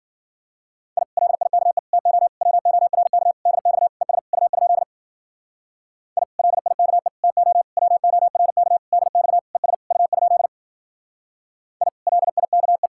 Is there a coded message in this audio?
Yes: Morse code from 1.0 to 12.9 s, heard as a keyed tone, about 50 wpm, 700 Hz. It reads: I LIKE TO WORK DX ES U? I LIKE TO WORK DX ES U? I LIKE